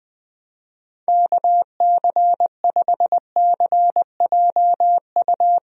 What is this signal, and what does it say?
Morse code, heard as a keyed tone, about 21 wpm, 700 Hz. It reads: KC5CJU
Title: USA Call Signs - 6 Characters - Repeated 3x - 20wpm